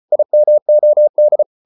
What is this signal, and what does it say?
Morse code, heard as a keyed tone, about 34 wpm, 600 Hz. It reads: IMOD
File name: Ferie QRQ 34wpm